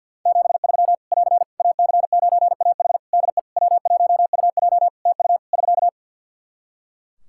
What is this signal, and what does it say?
Morse code, heard as a keyed tone, about 51 wpm, 700 Hz. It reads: 73 <AR> AC9AH DE W1UJ TU <SK>
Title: K1USN SST QSO Practice 50wpm